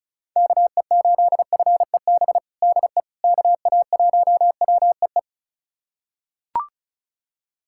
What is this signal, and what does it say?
Morse code, heard as a keyed tone, about 34 wpm, 700 Hz. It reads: KE8FEB DE KA1WEE
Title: USA Callsign DE Callsign - 6 Characters - 35wpm